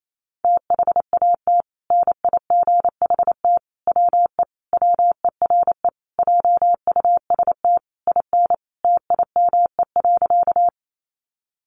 Morse code, heard as a keyed tone, about 28 wpm, 700 Hz. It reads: THAT NIGHT WE WERE JUST IN TIME.
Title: Sentences from Top 300 Words 28wpm